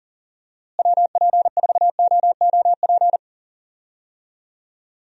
Morse code, heard as a keyed tone, about 40 wpm, 700 Hz. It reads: WP4OOP
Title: USA Call Signs - 6 Characters - Repeated 3x - 40wpm